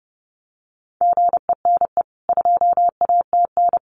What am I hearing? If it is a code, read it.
Morse code, heard as a keyed tone, about 30 wpm, 700 Hz. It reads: GENE 2ATN
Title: CWT Exchange 30wpm